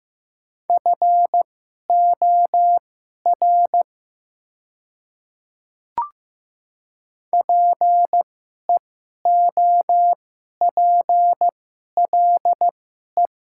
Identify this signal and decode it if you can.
Morse code, heard as a keyed tone, about 15 wpm, spaced as for 11 wpm, 700 Hz. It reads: FOR PEOPLE
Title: 2x Character Spacing - Top 100 Words - 15wpm